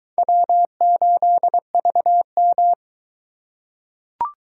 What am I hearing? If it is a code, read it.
Morse code, heard as a keyed tone, about 22 wpm, 700 Hz. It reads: W8VM